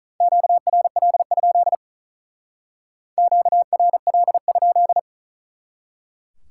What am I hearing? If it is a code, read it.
Morse code, heard as a keyed tone, about 43 wpm, 700 Hz. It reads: QRL? QRL?
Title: Speed-Racing Common QSO Phrases Encore - 35wpm